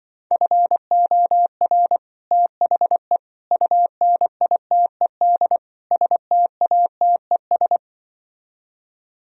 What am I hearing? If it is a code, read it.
Morse code, heard as a keyed tone, about 24 wpm, 700 Hz. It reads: FOR THE UNITED STATES
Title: Speed-Racing - Sets of 4 Words 20wpm